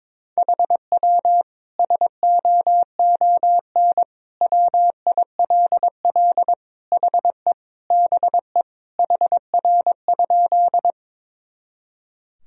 Morse code, heard as a keyed tone, about 22 wpm, 700 Hz. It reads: HW SOON WILL HE BE HR?
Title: Sentences from Top 400 Words and Abbreviations 22wpm